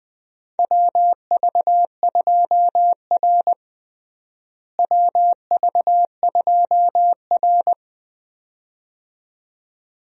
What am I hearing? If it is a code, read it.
Morse code, heard as a keyed tone, about 20 wpm, 700 Hz. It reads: WV2R WV2R
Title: USA Call Signs - 4 Characters - Repeated 2x - 20wpm